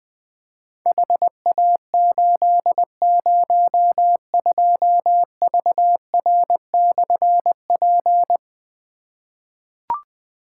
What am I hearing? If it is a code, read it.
Morse code, heard as a keyed tone, about 20 wpm, 700 Hz. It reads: HA802VR/P
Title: Difficult Call Signs 20wpm